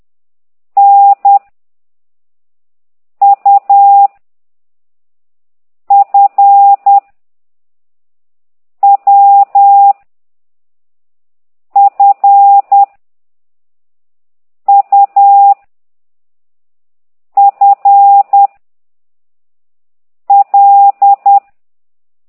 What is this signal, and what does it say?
Morse code, heard as a keyed tone, about 10 wpm, 800 Hz. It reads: N U F W F U F L
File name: ct2idw / exerc9